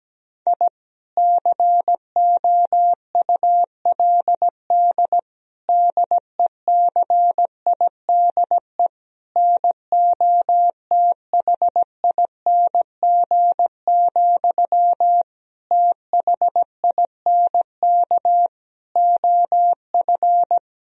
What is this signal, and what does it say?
Morse code, heard as a keyed tone, about 17 wpm, 700 Hz. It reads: I COULD DECIDE NOTHING, THINK OF
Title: Sentences from Top 1000 Words - Review Part 3 - 17wpm